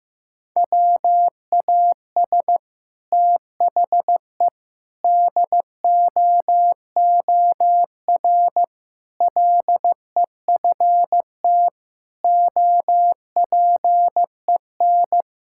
Morse code, heard as a keyed tone, about 15 wpm, 700 Hz. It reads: WAS THE DOOR LEFT OPEN
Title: Sentences from Top 1000 Words - Review Part 6 - 15wpm